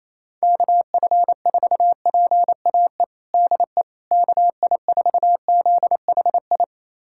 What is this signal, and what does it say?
Morse code, heard as a keyed tone, about 28 wpm, 700 Hz. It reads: KF4PAE DE KI4ZHI
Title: USA Callsign DE Callsign - 6 Characters - 28wpm